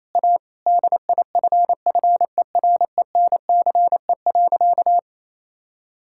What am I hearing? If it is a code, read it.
Morse code, heard as a keyed tone, about 28 wpm, 700 Hz. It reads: A DIFFERENCE.
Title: Sentences from Top 1000 Words - Review Part 5 - 28wpm